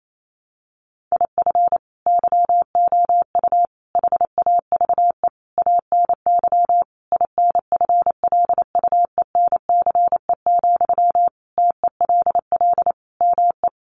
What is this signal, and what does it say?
Morse code, heard as a keyed tone, about 28 wpm, 700 Hz. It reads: IF YOU HAVE ANY INFLUENCE, TELL ME
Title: Sentences from Top 1100 Words 28wpm